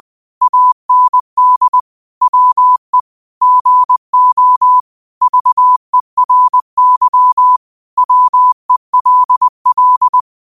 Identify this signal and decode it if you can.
Morse code, heard as a keyed tone, about 20 wpm, 1000 Hz. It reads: AND WE GO VERY WELL
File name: qotdfiles / 20wpm